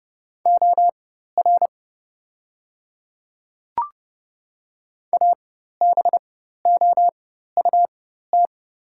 Morse code, heard as a keyed tone, about 30 wpm, spaced as for 14 wpm, 690 Hz. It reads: OR ABOUT
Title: 4x Character Spacing - Top 100 Words - 30wpm